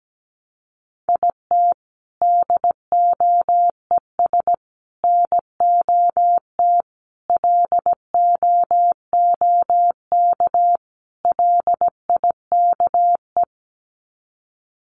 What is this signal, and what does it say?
Morse code, heard as a keyed tone, about 17 wpm, 700 Hz. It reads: IT DOES NOT LOOK LIKE
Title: Sets of 5 Words Encore2 17wpm